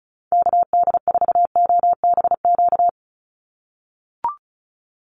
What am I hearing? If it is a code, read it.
Morse code, heard as a keyed tone, about 34 wpm, 700 Hz. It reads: KD4OBQ
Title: USA Call Signs 35wpm